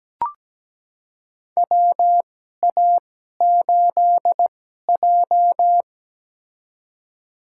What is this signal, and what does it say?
Morse code, heard as a keyed tone, about 17 wpm, spaced as for 12 wpm, 700 Hz. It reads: WA8J
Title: USA Call Signs - 4 Characters - 2x Character Spacing - 17wpm